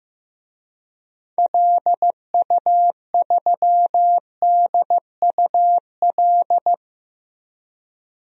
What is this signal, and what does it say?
Morse code, heard as a keyed tone, about 15 wpm, 700 Hz. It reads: LU3DUL
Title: DXCC-100 Call Signs - 6 Characters - 15wpm